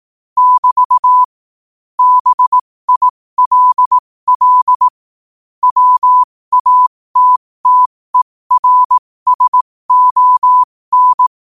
Morse code, heard as a keyed tone, about 18 wpm, spaced as for 15 wpm, 1000 Hz. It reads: <BT> BILL WATTERSON
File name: qotdfiles / 15wpm